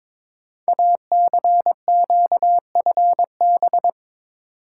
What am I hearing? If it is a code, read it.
Morse code, heard as a keyed tone, about 23 wpm, 700 Hz. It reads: ACQFB
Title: Code Groups - 5 Letters - 22wpm